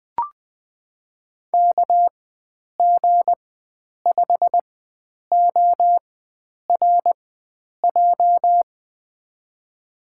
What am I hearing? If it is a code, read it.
Morse code, heard as a keyed tone, about 20 wpm, spaced as for 9 wpm, 700 Hz. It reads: KG5ORJ
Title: USA Call Signs - 6 Characters - 4x Character Spacing - 20wpm